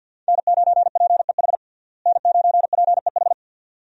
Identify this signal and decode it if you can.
Morse code, heard as a keyed tone, about 50 wpm, 700 Hz. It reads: N9PEH N9PEH
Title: USA Call Signs - 5 Characters - Repeated 2x - 50wpm